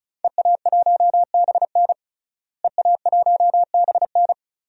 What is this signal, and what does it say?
Morse code, heard as a keyed tone, about 36 wpm, 700 Hz. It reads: EA1BN EA1BN
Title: DXCC-100 Call Signs - 5 Characters - Repeated 2x - 35wpm